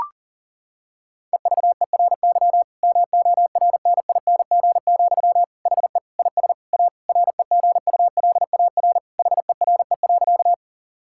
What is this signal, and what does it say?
Morse code, heard as a keyed tone, about 40 wpm, 700 Hz. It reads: EVERY MORNING, HE IS A REGULAR HERE.
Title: Sentences from Top 1000 Words - Review Part 6 - 40wpm